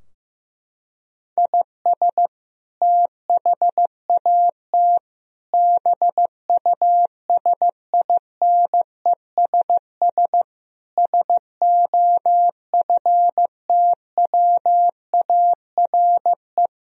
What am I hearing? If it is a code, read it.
Morse code, heard as a keyed tone, about 15 wpm, 700 Hz. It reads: IS THAT BUSINESS SOFTWARE
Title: Sentences from Top 1000 Words - Review Part 3 - 15wpm